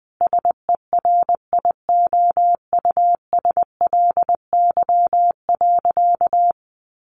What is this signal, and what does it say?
Morse code, heard as a keyed tone, about 20 wpm, 700 Hz. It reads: SERIOUSLY.
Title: Sentences from Top 1200 Words 20wpm